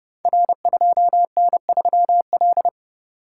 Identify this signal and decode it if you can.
Morse code, heard as a keyed tone, about 31 wpm, 700 Hz. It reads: R2N3L
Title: Code Groups - 5 Letter:Numbers - 30wpm